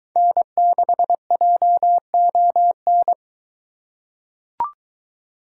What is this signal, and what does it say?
Morse code, heard as a keyed tone, about 22 wpm, 700 Hz. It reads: N6JON